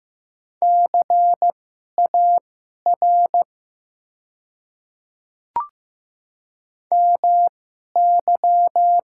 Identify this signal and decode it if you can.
Morse code, heard as a keyed tone, about 15 wpm, spaced as for 11 wpm, 700 Hz. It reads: CAR MY
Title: 2x Character Spacing - Top 300 Words - 15wpm